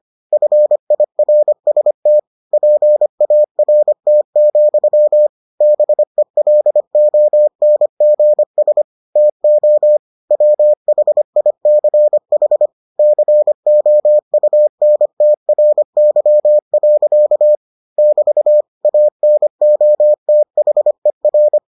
Morse code, heard as a keyed tone, about 25 wpm, 600 Hz. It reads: FIRST PART, BELONGS TO WHICH COUNTRY. <BT> ANOTHER